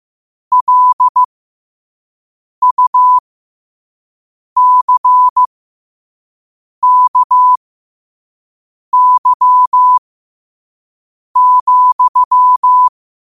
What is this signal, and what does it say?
Morse code, heard as a keyed tone, about 15 wpm, spaced as for 5 wpm, 1000 Hz. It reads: LUCKY,
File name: qotdfiles / 5wpm